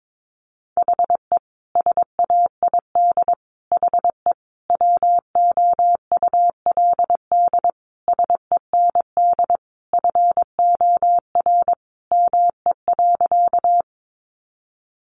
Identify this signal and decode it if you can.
Morse code, heard as a keyed tone, about 22 wpm, 700 Hz. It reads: HE SAID HE WOULD SEND FOR ME.
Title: Sentences from Top 1500 Words 22wpm